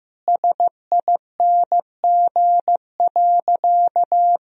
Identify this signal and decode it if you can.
Morse code, heard as a keyed tone, about 15 wpm, 700 Hz. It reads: SING.
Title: Sentences from Top 1000 Words - Review Part 2 - 15wpm